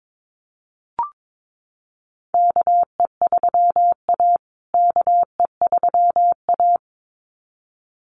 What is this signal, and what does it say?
Morse code, heard as a keyed tone, about 22 wpm, 700 Hz. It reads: KE3A KE3A
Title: USA Call Signs Repeated 2x 22wpm